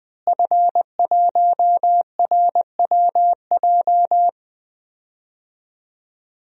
Morse code, heard as a keyed tone, about 20 wpm, 700 Hz. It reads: F1RWJ
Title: DXCC-100 Call Signs - 5 Characters - Repeated 3x - 20wpm